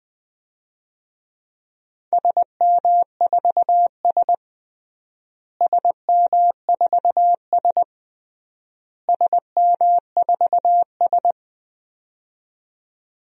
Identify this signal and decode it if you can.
Morse code, heard as a keyed tone, about 20 wpm, 700 Hz. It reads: SM4S SM4S SM4S
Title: DXCC-100 Call Signs - 4 Characters - Repeated 3x - 20wpm